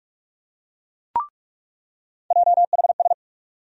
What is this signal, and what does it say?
Morse code, heard as a keyed tone, about 46 wpm, 700 Hz. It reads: JHS
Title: USA Call Sign Postfixes - 3 Letters 45wpm